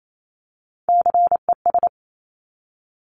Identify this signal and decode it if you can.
Morse code, heard as a keyed tone, about 28 wpm, 700 Hz. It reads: CES
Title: Top Three Letter Combinations in Words 28wpm